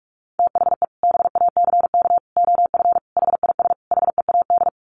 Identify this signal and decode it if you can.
Morse code, heard as a keyed tone, about 44 wpm, 700 Hz. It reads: THE BACK OF HIS HEAD
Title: Sets of 5 Words Encore 45wpm